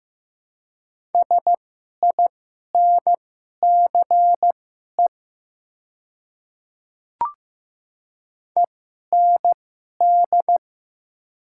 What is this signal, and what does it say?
Morse code, heard as a keyed tone, about 15 wpm, spaced as for 11 wpm, 700 Hz. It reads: SINCE END
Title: 2x Character Spacing - Top 300 Words - 15wpm